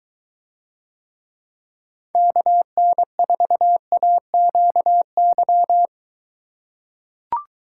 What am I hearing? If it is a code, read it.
Morse code, heard as a keyed tone, about 21 wpm, 700 Hz. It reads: KN4AQY